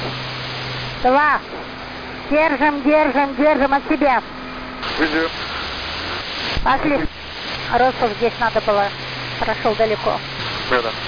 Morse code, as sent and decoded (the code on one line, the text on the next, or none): none